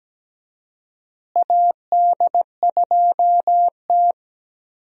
Morse code, heard as a keyed tone, about 17 wpm, 700 Hz. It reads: AD2T
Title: USA Call Signs - 4 Characters - Repeated 3x - 17wpm